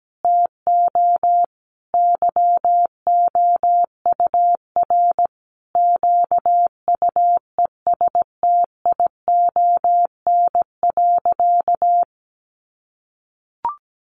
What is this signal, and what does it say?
Morse code, heard as a keyed tone, about 17 wpm, 700 Hz. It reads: TO YOUR QUESTION.
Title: Sentences from Top 800 Words 17wpm